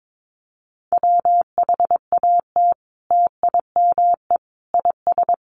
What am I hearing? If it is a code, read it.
Morse code, heard as a keyed tone, about 22 wpm, 700 Hz. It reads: WHAT TIME IS
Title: Sentences from Top 100 Words 22wpm